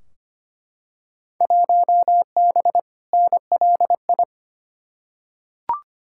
Morse code, heard as a keyed tone, about 25 wpm, 700 Hz. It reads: 1B NLI
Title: Field Day Exchanges Encore - 25wpm